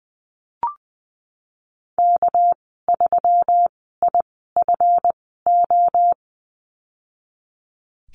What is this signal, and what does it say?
Morse code, heard as a keyed tone, about 20 wpm, spaced as for 15 wpm, 700 Hz. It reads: K3IFO
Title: 207 Lesson 33g - Letter j - Call Signs 20wpm